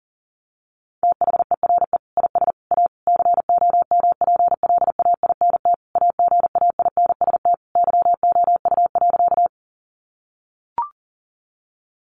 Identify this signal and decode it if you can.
Morse code, heard as a keyed tone, about 40 wpm, 700 Hz. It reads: THERE IS A COMPLAINT AGAINST YOU.
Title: Sentences from Top 1400 Words 40wpm